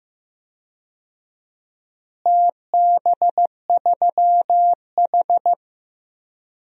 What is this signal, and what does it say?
Morse code, heard as a keyed tone, about 15 wpm, 700 Hz. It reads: TB3H
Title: DXCC-100 Call Signs - 4 Characters - Repeated 3x - 15wpm